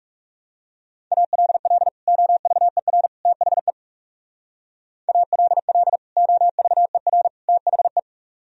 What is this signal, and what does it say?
Morse code, heard as a keyed tone, about 36 wpm, 700 Hz. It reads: ALL OVER THE ALL OVER THE
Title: Speed-Racing - Sets of 3 Words 30wpm